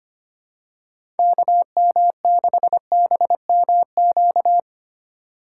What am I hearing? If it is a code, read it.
Morse code, heard as a keyed tone, about 25 wpm, 700 Hz. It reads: KM6BMQ
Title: USA Call Signs - 6 Characters - 3x Character Spacing - 25wpm